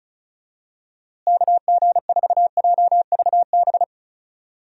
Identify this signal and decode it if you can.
Morse code, heard as a keyed tone, about 34 wpm, 700 Hz. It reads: KG4JVB
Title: USA Call Signs - 6 Characters - Repeated 2x - 35wpm